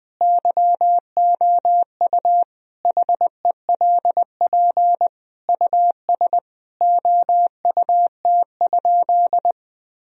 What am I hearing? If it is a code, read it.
Morse code, heard as a keyed tone, about 20 wpm, 700 Hz. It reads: YOU HELP US OUT?
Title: Sentences from Top 1000 Words - Review Part 3 - 20wpm